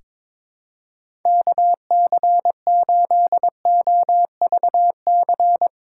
Morse code, heard as a keyed tone, about 22 wpm, 700 Hz. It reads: KC8OVC
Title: USA Call Signs - 6 Characters - 3x Character Spacing - 22wpm